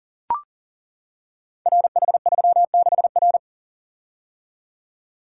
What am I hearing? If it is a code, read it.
Morse code, heard as a keyed tone, about 42 wpm, 690 Hz. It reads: RH36R
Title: Code Groups - 5 Letter:Numbers - 40wpm